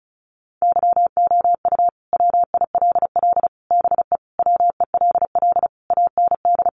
Morse code, heard as a keyed tone, about 34 wpm, 700 Hz. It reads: YOU WILL BE WELL AND
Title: Sentences from Top 1000 Words 35wpm